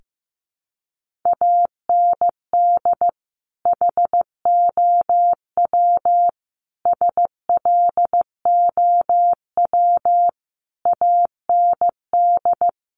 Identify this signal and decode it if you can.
Morse code, heard as a keyed tone, about 15 wpm, 700 Hz. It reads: AND HOW SLOW AND
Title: Sentences from Top 1500 Words 15wpm